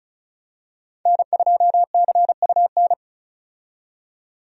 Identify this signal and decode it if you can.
Morse code, heard as a keyed tone, about 34 wpm, 700 Hz. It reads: N2CUN
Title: USA Call Signs - 5 Characters - 35wpm